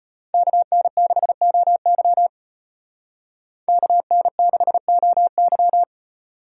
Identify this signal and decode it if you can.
Morse code, heard as a keyed tone, about 32 wpm, 700 Hz. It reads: KN6OY KN6OY